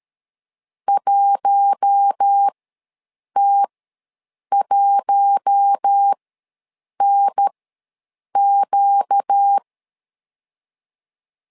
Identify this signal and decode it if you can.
Morse code, heard as a keyed tone, about 13 wpm, spaced as for 7 wpm, 800 Hz. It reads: 1T1NQ
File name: LEZIONI / 4 Q1 ws 12wpm 2129